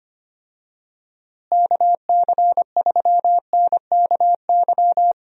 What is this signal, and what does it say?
Morse code, heard as a keyed tone, about 25 wpm, 700 Hz. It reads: KC3NKY
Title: USA Call Signs - 6 Characters - Repeated 3x - 25wpm